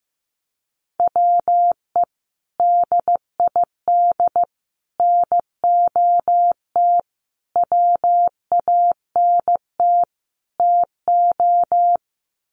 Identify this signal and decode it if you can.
Morse code, heard as a keyed tone, about 15 wpm, 700 Hz. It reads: WE DID NOT WANT TO
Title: Sets of 5 Words Encore 15wpm